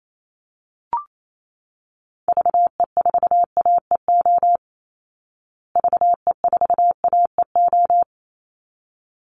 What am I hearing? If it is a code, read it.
Morse code, heard as a keyed tone, about 28 wpm, 700 Hz. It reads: VE4AEO VE4AEO
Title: Call Signs Repeated 3x - 28wpm